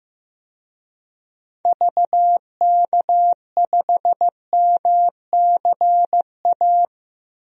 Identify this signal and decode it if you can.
Morse code, heard as a keyed tone, about 15 wpm, 690 Hz. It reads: VK5MCA
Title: DXCC-100 Call Signs - 6 Characters - 15wpm